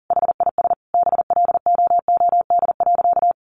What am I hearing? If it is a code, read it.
Morse code, heard as a keyed tone, about 40 wpm, 700 Hz. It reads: HIS BLOOD.
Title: Sentences from Top 1100 Words 40wpm